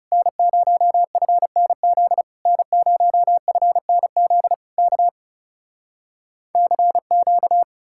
Morse code, heard as a keyed tone, about 37 wpm, 700 Hz. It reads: N0FNZ N0FNZ K CQ
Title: Speed-Racing Common QSO Phrases - 30wpm